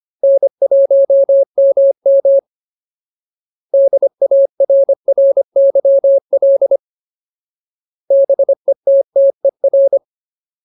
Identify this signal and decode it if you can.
Morse code, heard as a keyed tone, about 25 wpm, 550 Hz. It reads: N1MM DARRYL BETTER
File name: Ultimate Word List / 25WPM Ultimate